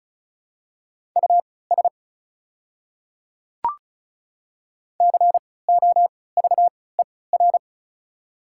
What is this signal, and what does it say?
Morse code, heard as a keyed tone, about 36 wpm, spaced as for 20 wpm, 700 Hz. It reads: US COVER
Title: 3x Character Spacing - Top 300 Words - 35wpm